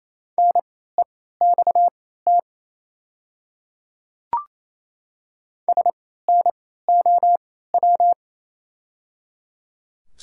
Morse code, heard as a keyed tone, about 27 wpm, spaced as for 16 wpm, 700 Hz. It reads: NEXT SNOW
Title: 3x Character Spacing - Top 500 Words - 28wpm